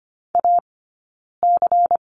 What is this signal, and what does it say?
Morse code, heard as a keyed tone, about 25 wpm, 700 Hz. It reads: A C
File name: course / 25:10wpm / 092 Lesson 16d - Letter c - Two Characters 25wpm